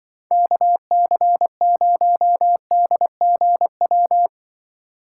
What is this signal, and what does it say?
Morse code, heard as a keyed tone, about 24 wpm, 700 Hz. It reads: KC0DGW
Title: Speed Racing - USA 2x3 Call Signs 20wpm